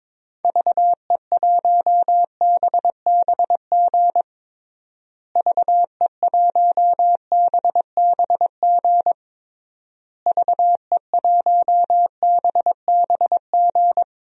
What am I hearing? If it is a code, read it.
Morse code, heard as a keyed tone, about 22 wpm, 700 Hz. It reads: VE1BBG VE1BBG VE1BBG
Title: DXCC-100 Call Signs - 6 Characters - Repeated 3x - 22wpm